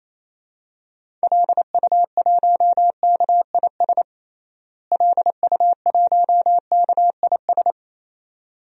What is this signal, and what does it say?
Morse code, heard as a keyed tone, about 28 wpm, 700 Hz. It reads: LU1KIS LU1KIS
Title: DXCC-100 Call Signs - 6 Characters - Repeated 2x - 28wpm